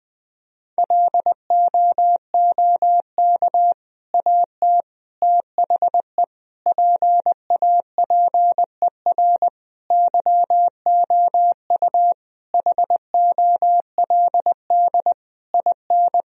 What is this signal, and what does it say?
Morse code, heard as a keyed tone, about 20 wpm, 700 Hz. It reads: LOOK AT THE PAPER YOU HOLD IN
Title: Sentences from Top 1000 Words - Review Part 2 - 20wpm